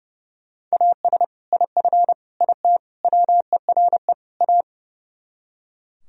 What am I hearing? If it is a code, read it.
Morse code, heard as a keyed tone, about 37 wpm, 700 Hz. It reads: AS IF IT WERE A
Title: Speed-Racing - Sets of 5 Words 30wpm